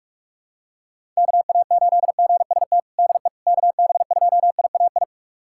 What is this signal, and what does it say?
Morse code, heard as a keyed tone, about 42 wpm, 700 Hz. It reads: KA8GST DE KB2IAI
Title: USA Callsign DE Callsign - 6 Characters - 45wpm